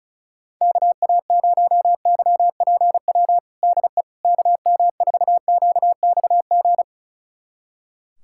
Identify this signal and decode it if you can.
Morse code, heard as a keyed tone, about 36 wpm, 700 Hz. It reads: KA0YPW DE KM4QXG
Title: USA Callsign DE Callsign - 6 Characters - 35wpm